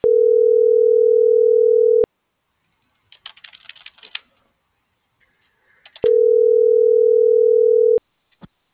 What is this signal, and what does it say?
Ringback tone (North America), 440 + 480 Hz.